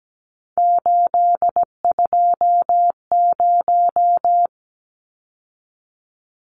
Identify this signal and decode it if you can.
Morse code, heard as a keyed tone, about 17 wpm, 700 Hz. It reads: 820
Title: Numbers - 3-Digit 17wpm